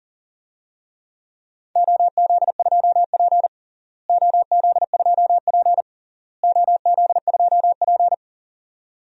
Morse code, heard as a keyed tone, about 40 wpm, 700 Hz. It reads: OZ2P OZ2P OZ2P
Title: DXCC-100 Call Signs - 4 Characters - Repeated 3x - 40wpm